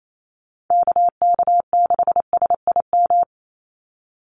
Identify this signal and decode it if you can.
Morse code, heard as a keyed tone, about 28 wpm, 700 Hz. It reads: KK6SIM
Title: USA Call Signs 28wpm